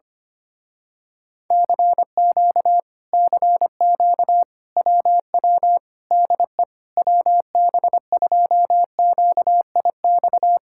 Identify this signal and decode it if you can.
Morse code, heard as a keyed tone, about 24 wpm, 700 Hz. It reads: CQ CQ WW DE WB2QIX
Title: Speed-Racing Common QSO Phrases - 25wpm